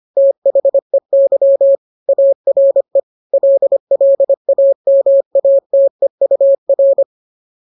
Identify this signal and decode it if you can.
Morse code, heard as a keyed tone, about 25 wpm, 550 Hz. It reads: THEY ARE LLAMATEUR